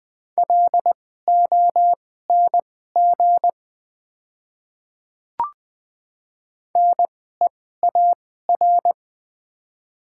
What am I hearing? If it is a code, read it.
Morse code, heard as a keyed tone, about 20 wpm, spaced as for 14 wpm, 700 Hz. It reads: LONG NEAR
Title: 2x Character Spacing - Top 200 Words - 20wpm